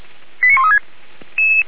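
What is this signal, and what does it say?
Five-tone selective call (ZVEI), heard as quick runs of five tones, tones of about 70 ms. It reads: 89136
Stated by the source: Five-tone ZVEI selective call received on SDR.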